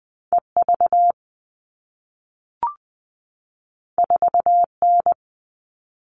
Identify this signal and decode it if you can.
Morse code, heard as a keyed tone, about 20 wpm, 700 Hz. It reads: EV 4N
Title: All Cut Numbers - 2-Digit 20wpm